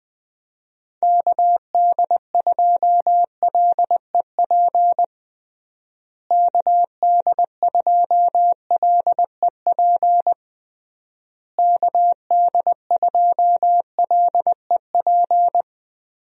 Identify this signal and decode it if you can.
Morse code, heard as a keyed tone, about 21 wpm, 700 Hz. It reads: KD2LEP KD2LEP KD2LEP
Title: USA Call Signs - 6 Characters - Repeated 3x - 20wpm